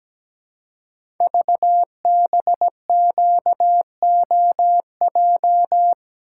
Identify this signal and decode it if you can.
Morse code, heard as a keyed tone, about 17 wpm, 700 Hz. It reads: VBQOJ
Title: Code Groups - 5 Letters - 17wpm